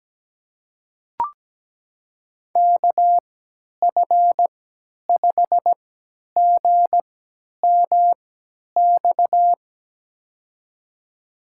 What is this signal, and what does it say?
Morse code, heard as a keyed tone, about 17 wpm, spaced as for 10 wpm, 700 Hz. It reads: KF5GMX
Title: USA Call Signs - 3x Character Spacing - 17wpm